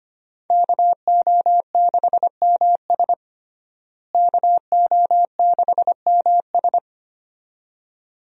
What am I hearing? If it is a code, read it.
Morse code, heard as a keyed tone, about 25 wpm, 700 Hz. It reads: KO6MS KO6MS
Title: USA Call Signs - 5 Characters - Repeated 2x - 25wpm